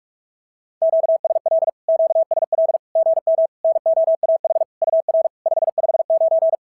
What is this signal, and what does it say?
Morse code, heard as a keyed tone, about 44 wpm, 650 Hz. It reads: QSL QSL GM NOAH UR 559
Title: SOTA QSO Practice 45wpm